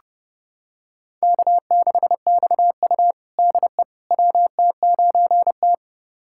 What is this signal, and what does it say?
Morse code, heard as a keyed tone, about 31 wpm, 700 Hz. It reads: K6XU DE WT9T
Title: USA Callsign DE Callsign - 4 Characters - 30wpm